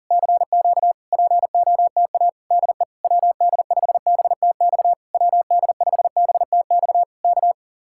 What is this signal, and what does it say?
Morse code, heard as a keyed tone, about 40 wpm, 700 Hz. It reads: CQ POTA DE WD5BTX WD5BTX K